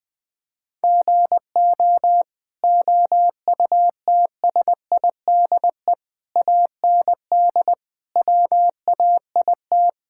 Morse code, heard as a keyed tone, about 21 wpm, 700 Hz. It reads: GO OUTSIDE AND WAIT
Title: Sentences from Top 1000 Words - Review Part 6 - 20wpm